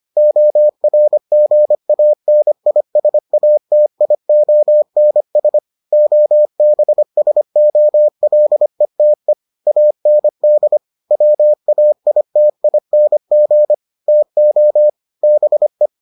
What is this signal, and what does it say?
Morse code, heard as a keyed tone, about 25 wpm, 600 Hz. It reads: ORGANISATIONS OBSOLETE AND WAITING TO BE